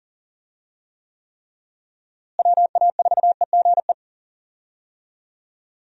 Morse code, heard as a keyed tone, about 40 wpm, 700 Hz. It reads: WA4EGE
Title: USA Call Signs - 6 Characters - Repeated 3x - 40wpm